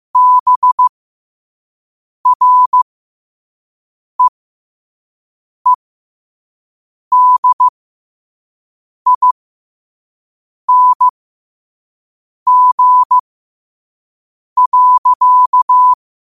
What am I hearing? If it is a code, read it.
Morse code, heard as a keyed tone, about 15 wpm, spaced as for 5 wpm, 1000 Hz. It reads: BREEDING.
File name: qotdfiles / 5wpm